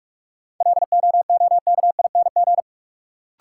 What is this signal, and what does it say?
Morse code, heard as a keyed tone, about 47 wpm, 700 Hz. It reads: LOOKING
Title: Sentences from Top 1000 Words - Review Part 6 - New Word Drill - 45wpm